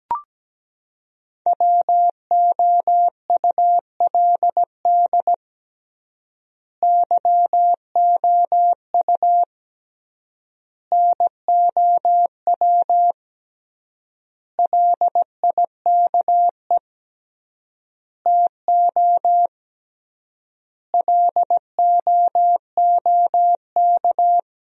Morse code, heard as a keyed tone, about 17 wpm, 700 Hz. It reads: WOULD YOU NOW LIKE TO LOOK
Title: 3x Word-Spacing - Sentences from Top 200 Words - 17wpm